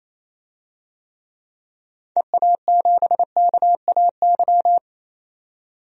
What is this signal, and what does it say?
Morse code, heard as a keyed tone, about 28 wpm, 700 Hz. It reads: EA7KAY
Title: DXCC-100 Call Signs - 6 Characters - 28wpm